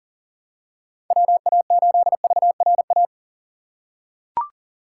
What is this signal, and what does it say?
Morse code, heard as a keyed tone, about 37 wpm, 700 Hz. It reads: WA8VRA
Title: USA Call Signs - 6 Characters - 3x Character Spacing - 40wpm